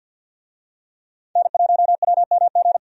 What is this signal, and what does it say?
Morse code, heard as a keyed tone, about 51 wpm, 700 Hz. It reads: N1WMG
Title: USA Call Signs - 5 Characters - 50wpm